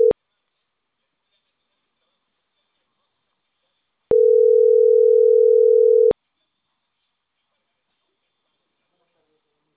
Ringback tone (North America), 440 + 480 Hz.